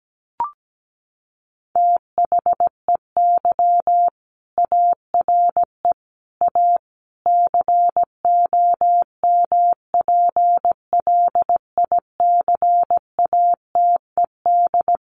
Morse code, heard as a keyed tone, about 17 wpm, 700 Hz. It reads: THEY ARE A COMPLICATED
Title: Sentences from Top 1200 Words 17wpm